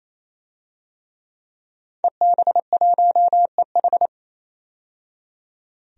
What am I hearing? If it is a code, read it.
Morse code, heard as a keyed tone, about 28 wpm, 700 Hz. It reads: EB1EH
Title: DXCC-100 Call Signs - 5 Characters - 28wpm